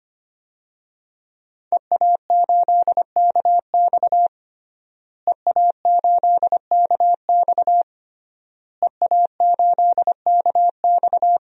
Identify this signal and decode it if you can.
Morse code, heard as a keyed tone, about 25 wpm, 700 Hz. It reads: EA8KX EA8KX EA8KX
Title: DXCC-100 Call Signs - 5 Characters - Repeated 3x - 25wpm